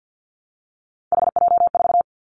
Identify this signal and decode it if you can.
Morse code, heard as a keyed tone, about 50 wpm, 700 Hz. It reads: HJ4
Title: Callsign Prefixes for Top 100 DXCC - 50wpm